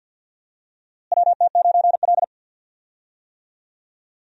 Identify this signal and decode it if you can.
Morse code, heard as a keyed tone, about 52 wpm, 700 Hz. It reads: WT9L
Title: USA Call Signs - 4 Characters - Repeated 3x - 50wpm